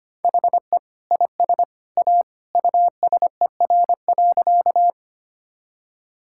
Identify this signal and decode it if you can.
Morse code, heard as a keyed tone, about 25 wpm, 700 Hz. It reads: HE IS A USER.
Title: Sentences from Top 1000 Words - Review Part 5 - 25wpm